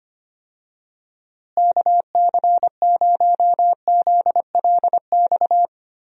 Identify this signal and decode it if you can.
Morse code, heard as a keyed tone, about 25 wpm, 700 Hz. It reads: KC0ZLX
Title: USA Call Signs - 6 Characters - Repeated 2x - 25wpm